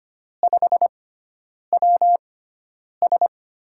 Morse code, heard as a keyed tone, about 25 wpm, spaced as for 9 wpm, 700 Hz. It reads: 5WS
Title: USA Call Signs - 4 Characters - 6x Character Spacing - 25wpm